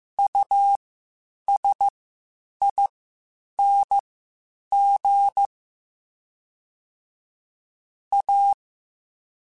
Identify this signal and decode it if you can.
Morse code, heard as a keyed tone, about 15 wpm, spaced as for 8 wpm, 800 Hz. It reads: USING A
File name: (spaced) / QSOs Spaced-mp3s / WD0FBT 1 @13 sp